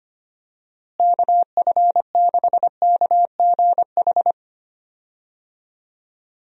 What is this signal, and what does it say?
Morse code, heard as a keyed tone, about 25 wpm, 700 Hz. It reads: KF6KGH
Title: USA Call Signs - 6 Characters - Repeated 3x - 25wpm